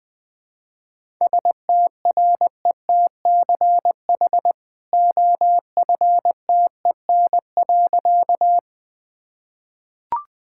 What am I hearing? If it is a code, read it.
Morse code, heard as a keyed tone, about 20 wpm, 700 Hz. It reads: STRETCH OFTEN.
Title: Easy Sentences with Hints - 20wpm